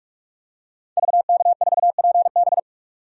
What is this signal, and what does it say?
Morse code, heard as a keyed tone, about 48 wpm, 700 Hz. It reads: VK4PB
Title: DXCC-100 Call Signs - 5 Characters - 45wpm